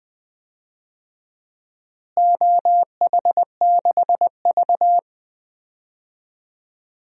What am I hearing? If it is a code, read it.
Morse code, heard as a keyed tone, about 21 wpm, 700 Hz. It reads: OH6V
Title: DXCC-100 Call Signs - 4 Characters - Repeated 2x - 20wpm